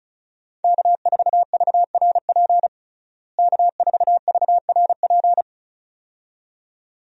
Morse code, heard as a keyed tone, about 34 wpm, 700 Hz. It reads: K4VRP K4VRP
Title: USA Call Signs - 5 Characters - Repeated 2x - 35wpm